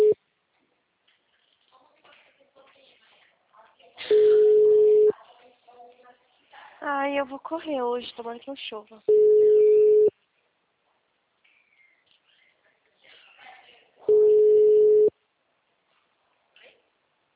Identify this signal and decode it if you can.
Ringback tone (Europe, 425 Hz), with speech.
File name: out- -2041- - - .71183